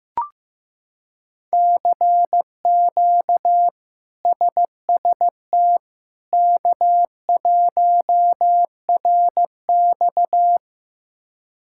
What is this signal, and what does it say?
Morse code, heard as a keyed tone, about 15 wpm, 700 Hz. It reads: CQ SST K1RX
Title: K1USN SST QSO Practice - 15wpm